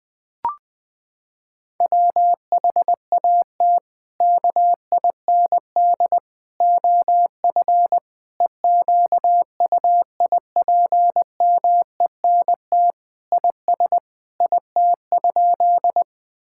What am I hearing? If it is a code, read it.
Morse code, heard as a keyed tone, about 20 wpm, 700 Hz. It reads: WHAT KIND OF EQUIPMENT IS IT?
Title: Sentences from Top 1000 Words - Review Part 5 - 20wpm